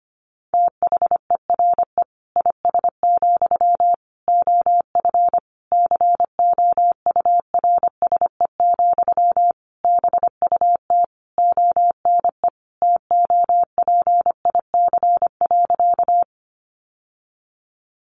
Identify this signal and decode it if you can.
Morse code, heard as a keyed tone, about 25 wpm, 700 Hz. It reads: THERE IS, OF COURSE, BUT ONE TOPIC.
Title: Sentences from Top 1200 Words 25wpm